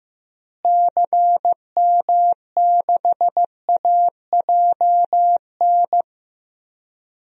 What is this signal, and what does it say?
Morse code, heard as a keyed tone, about 15 wpm, 700 Hz. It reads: CM6AJN
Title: DXCC-100 Call Signs - 6 Characters - Repeated 3x - 15wpm